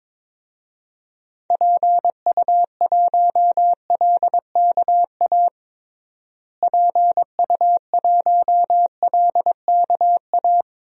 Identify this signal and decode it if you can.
Morse code, heard as a keyed tone, about 22 wpm, 700 Hz. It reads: PU1LKA PU1LKA
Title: DXCC-100 Call Signs - 6 Characters - Repeated 3x - 22wpm